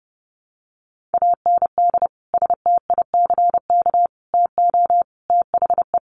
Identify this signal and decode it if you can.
Morse code, heard as a keyed tone, about 30 wpm, 700 Hz. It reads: AND STICK TO THE
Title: Sentences from Top 1200 Words 30wpm